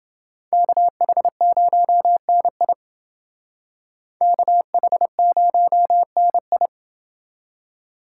Morse code, heard as a keyed tone, about 30 wpm, 700 Hz. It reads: KH0NI KH0NI